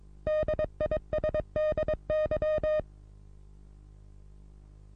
Morse code, heard as a keyed tone, about 30 wpm, 620 Hz. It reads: DISDY
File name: repos / Morsekurs / DJ4UF Morsen27